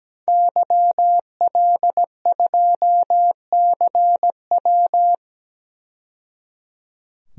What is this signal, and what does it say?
Morse code, heard as a keyed tone, about 17 wpm, 700 Hz. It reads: YL2CW
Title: DXCC-100 Call Signs - 5 Characters - 17wpm